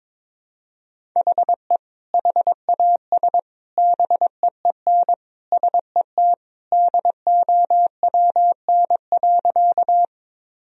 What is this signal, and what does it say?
Morse code, heard as a keyed tone, about 22 wpm, 700 Hz. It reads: HE HAS BEEN SET DOWN.
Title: Sentences from Top 1000 Words - Review Part 1 - 22wpm